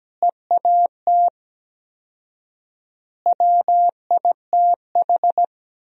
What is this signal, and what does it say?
Morse code, heard as a keyed tone, about 17 wpm, 700 Hz. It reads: EAT WITH
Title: 4x Word-Spacing - Sentences from Top 300 Words - 17wpm